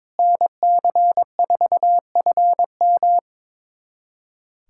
Morse code, heard as a keyed tone, about 22 wpm, 700 Hz. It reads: NC4FM
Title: USA Callsign DE Callsign - 5 Characters - 22wpm